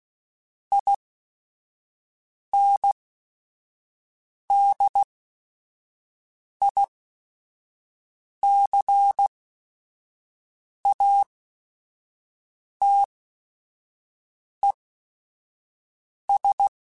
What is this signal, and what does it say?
Morse code, heard as a keyed tone, about 16 wpm, spaced as for 5 wpm, 800 Hz. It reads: INDICATES
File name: Technical Subjects / Technical Subjects-mp3s / XC @5